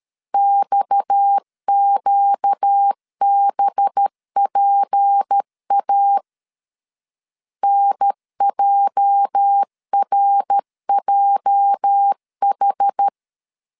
Morse code, heard as a keyed tone, about 13 wpm, 800 Hz. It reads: XQBPA NJRJH
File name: LEZIONI / 8 YJ45 / 8 YJ 12wpm 2235